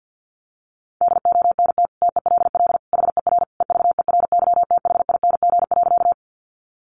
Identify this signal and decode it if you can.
Morse code, heard as a keyed tone, about 50 wpm, 700 Hz. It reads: DONT TELL HER EVERYTHING.
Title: Sentences from Top 700 Words 50wpm